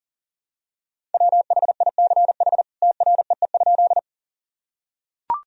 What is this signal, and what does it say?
Morse code, heard as a keyed tone, about 40 wpm, 700 Hz. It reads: WHICH TREE?
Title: Sentences from Top 1000 Words - Review Part 1 - 40wpm